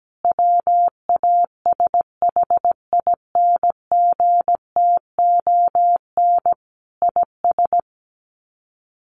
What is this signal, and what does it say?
Morse code, heard as a keyed tone, about 17 wpm, 700 Hz. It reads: WASHINGTON IS
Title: Sets of 5 Words Encore2 17wpm